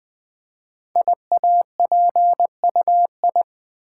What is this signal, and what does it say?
Morse code, heard as a keyed tone, about 20 wpm, 700 Hz. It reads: IAPUI
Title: Code Groups - 5 Letters - 20wpm